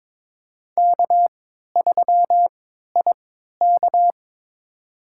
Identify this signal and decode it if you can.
Morse code, heard as a keyed tone, about 22 wpm, spaced as for 12 wpm, 700 Hz. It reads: K3IK
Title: USA Call Signs - 4 Characters - 3x Character Spacing - 22wpm